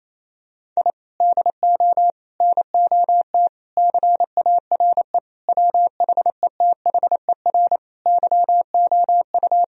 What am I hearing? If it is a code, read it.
Morse code, heard as a keyed tone, about 28 wpm, 700 Hz. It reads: I DO NOT CARE WHETHER YOU
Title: Sentences from Top 1000 Words - Review Part 3 - 28wpm